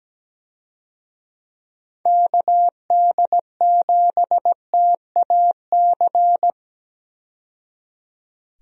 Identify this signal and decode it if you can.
Morse code, heard as a keyed tone, about 17 wpm, 700 Hz. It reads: KD7TAC
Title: USA Call Signs - 6 Characters - 17wpm